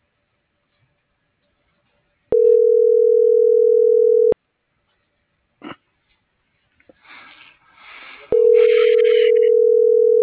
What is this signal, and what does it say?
Ringback tone (North America), 440 + 480 Hz.